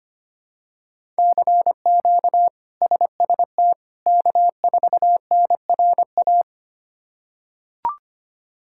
Morse code, heard as a keyed tone, about 25 wpm, 700 Hz. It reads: CQ SST K4NRA
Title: K1USN SST QSO Practice - 25wpm